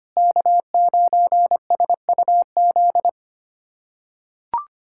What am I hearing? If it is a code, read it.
Morse code, heard as a keyed tone, about 25 wpm, 700 Hz. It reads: K9SUZ
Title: USA Callsign DE Callsign - 5 Characters - 25wpm